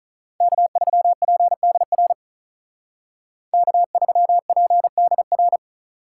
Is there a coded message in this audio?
Yes: Morse code from 0.4 to 5.6 s, heard as a keyed tone, about 45 wpm, 700 Hz. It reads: K3PDR K3PDR